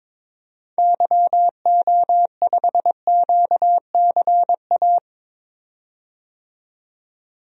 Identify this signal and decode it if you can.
Morse code, heard as a keyed tone, about 22 wpm, 700 Hz. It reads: YO5QCA
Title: DXCC-100 Call Signs - 6 Characters - 22wpm